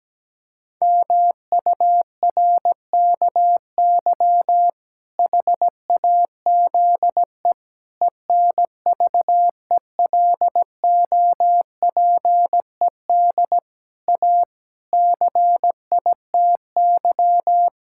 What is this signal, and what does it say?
Morse code, heard as a keyed tone, about 17 wpm, 700 Hz. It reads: MURKY HAZE ENVELOPED A CITY
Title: Pangram Challenge v2 - 17wpm